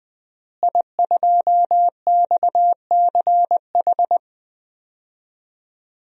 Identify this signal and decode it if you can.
Morse code, heard as a keyed tone, about 20 wpm, 700 Hz. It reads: I2XCH
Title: DXCC-100 Call Signs - 5 Characters - Repeated 3x - 20wpm